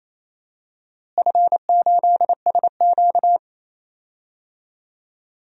Morse code, heard as a keyed tone, about 28 wpm, 700 Hz. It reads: F8SQ
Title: DXCC-100 Call Signs - 4 Characters - Repeated 2x - 28wpm